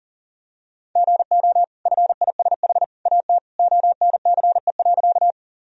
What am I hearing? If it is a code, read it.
Morse code, heard as a keyed tone, about 40 wpm, 700 Hz. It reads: GO FISH AT ONCE.
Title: Sentences from Top 1000 Words - Review Part 1 - 40wpm